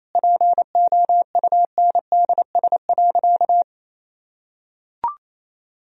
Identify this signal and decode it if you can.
Morse code, heard as a keyed tone, about 28 wpm, 700 Hz. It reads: POUNDS.
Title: Sentences from Top 1000 Words - Review Part 4 - 28wpm